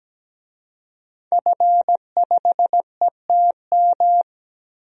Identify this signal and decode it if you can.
Morse code, heard as a keyed tone, about 17 wpm, 700 Hz. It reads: F5ETM
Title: DXCC-100 Call Signs - 5 Characters - Repeated 2x - 17wpm